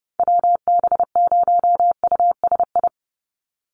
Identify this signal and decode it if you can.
Morse code, heard as a keyed tone, about 30 wpm, 700 Hz. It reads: WB0USI
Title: NA SOTA Activator Callsigns 30wpm